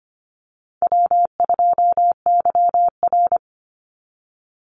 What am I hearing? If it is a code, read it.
Morse code, heard as a keyed tone, about 25 wpm, 700 Hz. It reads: W2YR
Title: Call Signs Repeated 2x - 25wpm